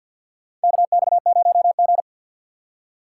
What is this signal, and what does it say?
Morse code, heard as a keyed tone, about 49 wpm, 700 Hz. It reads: KX0G
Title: USA Call Signs - 4 Characters - 3x Character Spacing - 50wpm